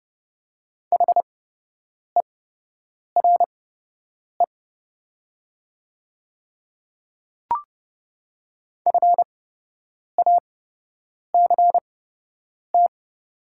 Morse code, heard as a keyed tone, about 30 wpm, spaced as for 8 wpm, 700 Hz. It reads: HERE FACT
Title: 8x Character Spacing - Top 500 Words - 30wpm